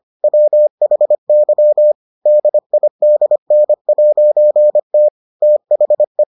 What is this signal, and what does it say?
Morse code, heard as a keyed tone, about 25 wpm, 600 Hz. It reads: WHY DIDN'T THE